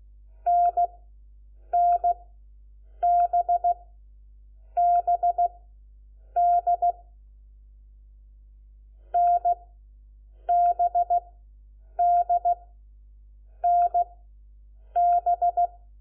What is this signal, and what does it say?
Morse code, heard as a keyed tone, about 16 wpm, spaced as for 7 wpm, 700 Hz. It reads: NNBBD NBDNB
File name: Lecciones / CLASE Nº4